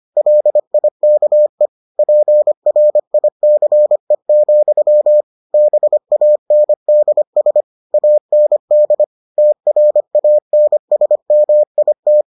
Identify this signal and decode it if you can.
Morse code, heard as a keyed tone, about 25 wpm, 600 Hz. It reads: LIKE PRICE, BANDS AND TRANSMIT